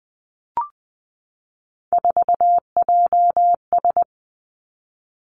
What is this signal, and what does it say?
Morse code, heard as a keyed tone, about 20 wpm, 700 Hz. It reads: 4JS
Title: Three Challenging Letter-Number Combinations 20wpm